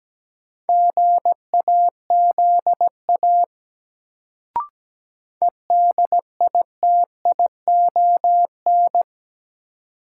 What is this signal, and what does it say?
Morse code, heard as a keyed tone, about 17 wpm, 700 Hz. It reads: GAZA EDITION
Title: Sets of 5 Words - New Word Drill - 17wpm